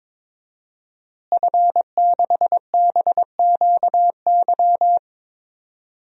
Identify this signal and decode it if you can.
Morse code, heard as a keyed tone, about 22 wpm, 700 Hz. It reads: F6BQY
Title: DXCC-100 Call Signs - 5 Characters - Repeated 3x - 22wpm